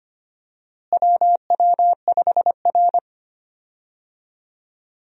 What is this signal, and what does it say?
Morse code, heard as a keyed tone, about 25 wpm, 700 Hz. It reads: WW5R
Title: USA Call Signs - 4 Characters - Repeated 2x - 25wpm